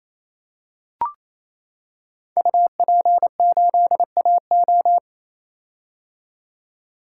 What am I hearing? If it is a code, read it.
Morse code, heard as a keyed tone, about 28 wpm, 700 Hz. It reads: UP8AO
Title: Code Groups - 5 Letter:Numbers - 28wpm